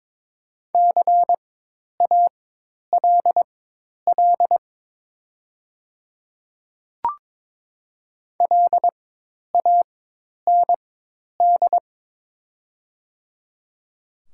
Morse code, heard as a keyed tone, about 22 wpm, spaced as for 10 wpm, 700 Hz. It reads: CALL LAND
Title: 4x Character Spacing - Top 300 Words - 22wpm